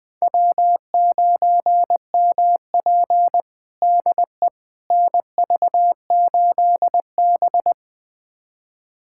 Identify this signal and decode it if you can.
Morse code, heard as a keyed tone, about 20 wpm, 700 Hz. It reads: W9MP DE NV8B
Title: USA Callsign DE Callsign - 4 Characters - 20wpm